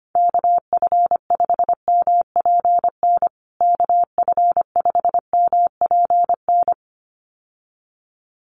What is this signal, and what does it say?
Morse code, heard as a keyed tone, about 25 wpm, 700 Hz. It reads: KF5MPN KF5MPN
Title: USA Call Signs Repeated 2x 25wpm